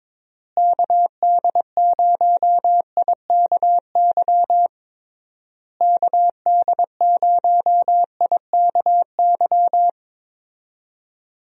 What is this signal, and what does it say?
Morse code, heard as a keyed tone, about 22 wpm, 700 Hz. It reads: KD0IKY KD0IKY
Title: USA Call Signs - 6 Characters - Repeated 3x - 22wpm